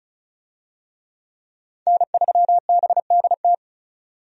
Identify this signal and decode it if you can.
Morse code, heard as a keyed tone, about 34 wpm, 700 Hz. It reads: N3BDT
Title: USA Call Signs - 5 Characters - 35wpm